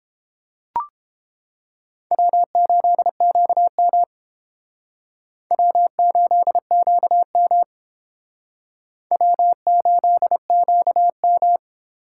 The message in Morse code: W8QM W8QM W8QM

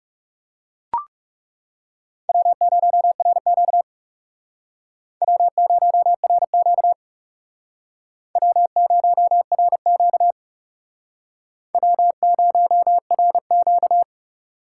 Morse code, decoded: W0RQ W0RQ W0RQ W0RQ

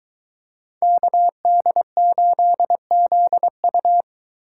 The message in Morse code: KD8ZU

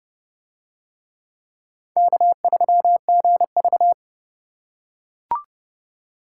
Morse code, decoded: K3GV

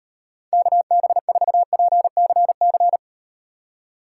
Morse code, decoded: KB4PCC

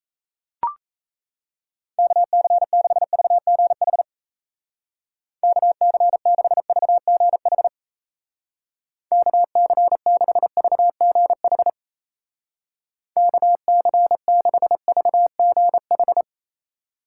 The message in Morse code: KC6VGH KC6VGH KC6VGH KC6VGH

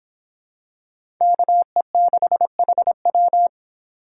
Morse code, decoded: KE6HW

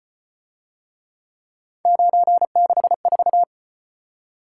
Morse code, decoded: 964

